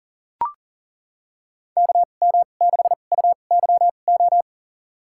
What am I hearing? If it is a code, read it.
Morse code, heard as a keyed tone, about 40 wpm, spaced as for 29 wpm, 700 Hz. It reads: KM6UYO
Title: USA Call Signs - 2x Character Spacing - 40wpm